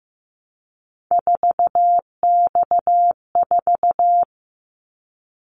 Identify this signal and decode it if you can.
Morse code, heard as a keyed tone, about 15 wpm, 700 Hz. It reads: 4X4
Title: Callsign Prefixes for Top 50 DXCC - 15wpm